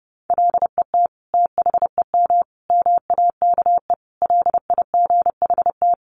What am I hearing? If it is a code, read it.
Morse code, heard as a keyed tone, about 30 wpm, 700 Hz. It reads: LET THEM MAKE LIGHT
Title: Sentences from Top 1300 Words 30wpm